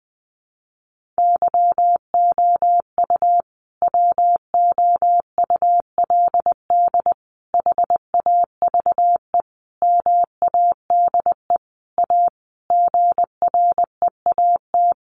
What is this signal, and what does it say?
Morse code, heard as a keyed tone, about 20 wpm, 700 Hz. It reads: YOU WOULD HAVE MADE A GREAT
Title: Sentences from Top 1500 Words 20wpm